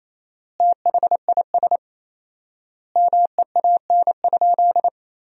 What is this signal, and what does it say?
Morse code, heard as a keyed tone, about 28 wpm, 700 Hz. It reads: THIS MEAN?
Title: 4x Word-Spacing - Sentences from Top 200 Words - 28wpm